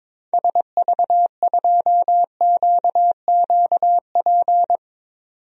Morse code, heard as a keyed tone, about 22 wpm, 700 Hz. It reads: SV2QQP
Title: DXCC-100 Call Signs - 6 Characters - 22wpm